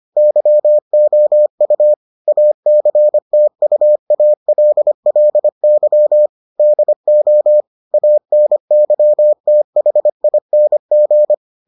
Morse code, heard as a keyed tone, about 25 wpm, 600 Hz. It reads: YOU ACTUALLY DO ANYTHING